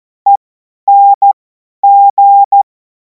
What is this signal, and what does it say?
Morse code, heard as a keyed tone, about 14 wpm, spaced as for 10 wpm, 800 Hz. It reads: ENG
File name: Morsekurs / Witze / Witz 03. März 2026 50 F70